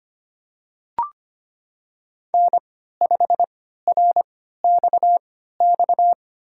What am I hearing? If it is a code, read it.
Morse code, heard as a keyed tone, about 25 wpm, spaced as for 14 wpm, 700 Hz. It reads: N5RXX
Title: USA Call Signs - 3x Character Spacing - 25wpm